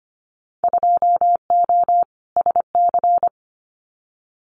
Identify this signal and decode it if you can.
Morse code, heard as a keyed tone, about 25 wpm, 700 Hz. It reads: 2O SC
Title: Winter Field Day Exchanges 25wpm